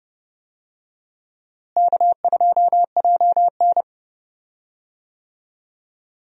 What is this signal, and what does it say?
Morse code, heard as a keyed tone, about 31 wpm, 700 Hz. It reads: K2JN
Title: USA Call Signs - 4 Characters - Repeated 2x - 30wpm